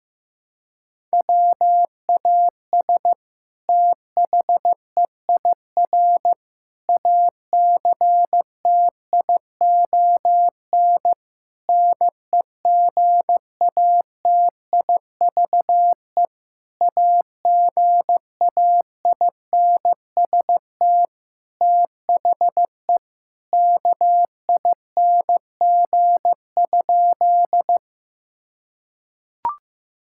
Morse code, heard as a keyed tone, about 15 wpm, 700 Hz. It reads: WAS THEIR ACTION NEGATIVE AGAINST THE KING?
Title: Sentences from Top 1000 Words - Review Part 6 - 15wpm